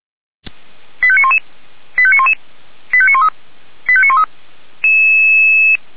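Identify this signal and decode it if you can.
Five-tone selective call (ZVEI), heard as quick runs of five tones, tones of about 70 ms. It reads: 85911 85911 85913 85913
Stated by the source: SDR-received ZVEI five-tone call.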